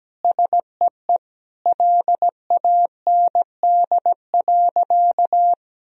Morse code, heard as a keyed tone, about 17 wpm, 700 Hz. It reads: SEE LAND.
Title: Sentences from Top 1000 Words - Review Part 2 - 17wpm